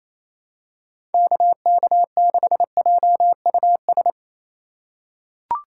Morse code, heard as a keyed tone, about 28 wpm, 700 Hz. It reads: KK6JUS
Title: USA Call Signs - 6 Characters - 3x Character Spacing - 28wpm